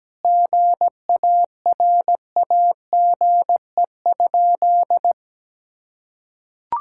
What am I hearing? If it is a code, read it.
Morse code, heard as a keyed tone, about 17 wpm, 700 Hz. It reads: GARAGE?
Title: Sentences from Top 1600 Words 17wpm